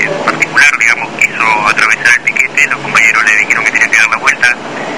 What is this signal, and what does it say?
Emergency Alert System (SAME) tones, heard as harsh buzzing bursts, with speech. It is heard from 0.5 to 3.8 s.